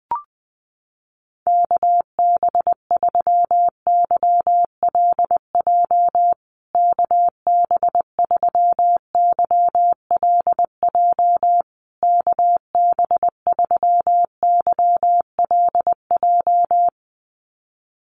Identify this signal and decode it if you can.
Morse code, heard as a keyed tone, about 20 wpm, 700 Hz. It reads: KB3YLJ KB3YLJ KB3YLJ
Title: USA Call Signs Repeated 3x 20wpm